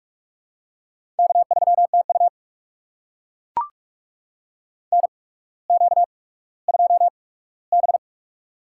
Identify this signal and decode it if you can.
Morse code, heard as a keyed tone, about 42 wpm, spaced as for 19 wpm, 700 Hz. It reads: K3TU NQ2B
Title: USA Call Signs - 8x Character Spacing - 45wpm